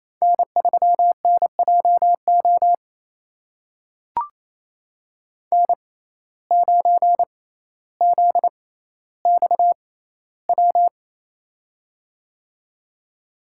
Morse code, heard as a keyed tone, about 28 wpm, spaced as for 14 wpm, 700 Hz. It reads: N3NJO N9ZXW
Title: USA Call Signs - 5 Characters - 6x Character Spacing - 28wpm